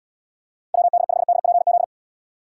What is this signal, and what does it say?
Morse code, heard as a keyed tone, about 71 wpm, 700 Hz. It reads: WD5DPZ